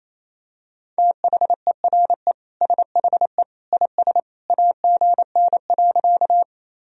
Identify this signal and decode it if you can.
Morse code, heard as a keyed tone, about 28 wpm, 700 Hz. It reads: THERE SHE IS AGN.
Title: Sentences from Top 200 Words and Abbreviations 28wpm